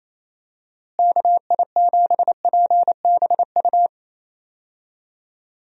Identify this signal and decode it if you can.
Morse code, heard as a keyed tone, about 28 wpm, 700 Hz. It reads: KI7PBU
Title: USA Call Signs - 6 Characters - Repeated 3x - 28wpm